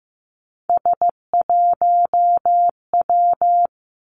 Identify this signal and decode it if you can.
Morse code, heard as a keyed tone, about 15 wpm, 700 Hz. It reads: S1W
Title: Three Challenging Letter-Number Combinations 15wpm